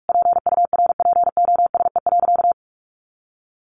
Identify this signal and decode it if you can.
Morse code, heard as a keyed tone, about 44 wpm, 700 Hz. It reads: PURPOSE.
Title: Sentences from Top 800 Words 45wpm